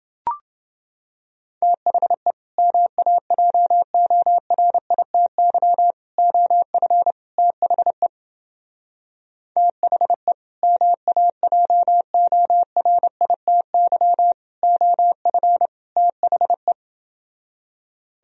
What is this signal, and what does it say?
Morse code, heard as a keyed tone, about 29 wpm, 700 Hz. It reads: THE MAJORITY OF THE THE MAJORITY OF THE